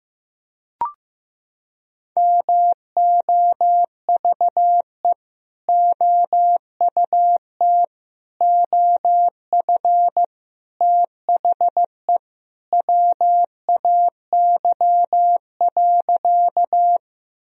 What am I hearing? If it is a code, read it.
Morse code, heard as a keyed tone, about 16 wpm, 700 Hz. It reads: MOVE OUT OF THE WAY.
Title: Sentences from Top 1000 Words - Review Part 1 - 15wpm